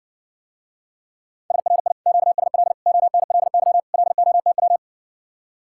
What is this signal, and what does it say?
Morse code, heard as a keyed tone, about 60 wpm, 700 Hz. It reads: SRI QSL ONLY LOTW